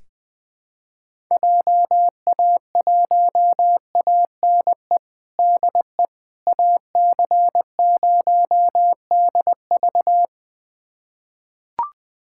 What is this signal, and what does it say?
Morse code, heard as a keyed tone, about 21 wpm, 700 Hz. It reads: JA1ANE DE AC0DV
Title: USA Callsign DE Callsign 20wpm